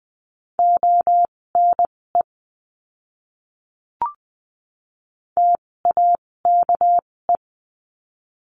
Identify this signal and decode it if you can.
Morse code, heard as a keyed tone, about 20 wpm, spaced as for 16 wpm, 700 Hz. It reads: ONE TAKE
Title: 256 Lesson 42a - Words 12-20wpm